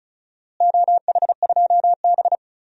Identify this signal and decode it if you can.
Morse code, heard as a keyed tone, about 36 wpm, 690 Hz. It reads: OH2B
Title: DXCC-100 Call Signs - 4 Characters - Repeated 2x - 35wpm